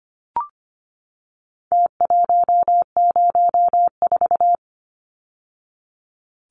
Morse code, heard as a keyed tone, about 25 wpm, 700 Hz. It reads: T104
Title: Numbers - Contest Style - 4-Digit 25wpm